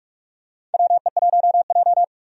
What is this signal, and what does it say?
Morse code, heard as a keyed tone, about 47 wpm, 700 Hz. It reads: WE1J
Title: USA Call Signs - 4 Characters - 45wpm